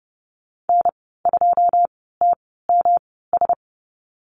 Morse code, heard as a keyed tone, about 30 wpm, spaced as for 17 wpm, 700 Hz. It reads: N2TMS
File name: Lesson 19g - Letter w - Call Signs 30wpm